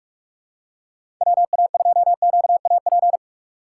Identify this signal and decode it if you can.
Morse code, heard as a keyed tone, about 46 wpm, 700 Hz. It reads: WA2QAP
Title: USA Call Signs - 6 Characters - 4x Character Spacing - 45wpm